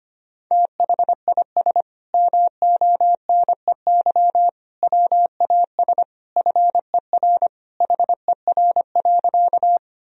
Morse code, heard as a keyed tone, about 25 wpm, 700 Hz. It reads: THIS MONEY WAS FER HER.
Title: Sentences from Top 400 Words and Abbreviations 25wpm